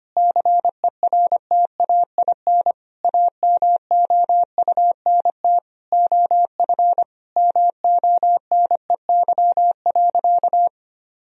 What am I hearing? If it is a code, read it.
Morse code, heard as a keyed tone, about 25 wpm, 700 Hz. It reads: CERTAIN AMOUNT OF MONEY.
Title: Sentences from Top 1000 Words - Review Part 4 - 25wpm